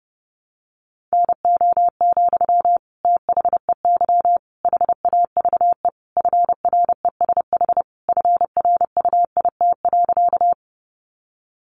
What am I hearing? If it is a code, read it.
Morse code, heard as a keyed tone, about 30 wpm, 700 Hz. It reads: NO, THEY HAVE FRESH FRUIT.
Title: Sentences from Top 1300 Words 30wpm